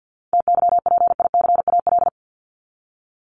Morse code, heard as a keyed tone, about 48 wpm, 700 Hz. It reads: TYPICAL
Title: LY Suffix - Top 100 Words with Base Words 50wpm